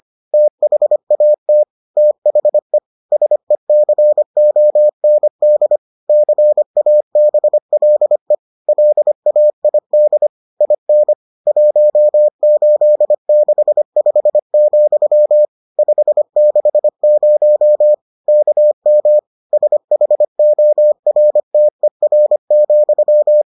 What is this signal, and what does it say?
Morse code, heard as a keyed tone, about 25 wpm, 600 Hz. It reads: THAT THE SECOND CABLE LAID IN 1865, 560 KM SHORTER,